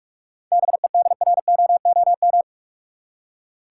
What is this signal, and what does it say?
Morse code, heard as a keyed tone, about 46 wpm, 700 Hz. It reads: BEDROOM
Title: Seven Letter Words - 45wpm